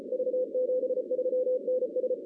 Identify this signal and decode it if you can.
Morse code, heard as a keyed tone, about 34 wpm, 500 Hz. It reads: V73NS
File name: 2016 arrl dx cw / 2016-ARRL-DX-CW-K5TR- - 0-25